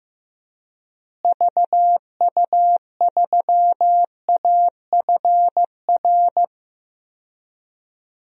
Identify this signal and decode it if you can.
Morse code, heard as a keyed tone, about 15 wpm, 700 Hz. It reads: VU3AFR
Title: DXCC-100 Call Signs - 6 Characters - Repeated 3x - 15wpm